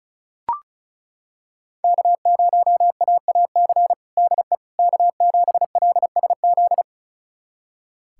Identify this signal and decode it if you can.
Morse code, heard as a keyed tone, about 36 wpm, 700 Hz. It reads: K0AAC DE K7LSZ
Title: USA Callsign DE Callsign - 5 Characters - 35wpm